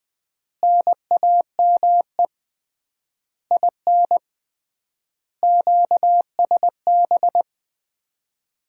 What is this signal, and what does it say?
Morse code, heard as a keyed tone, about 20 wpm, 700 Hz. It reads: NAME IN QSB
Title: Common QSO Phrases - 3x Word-Spacing - 20wpm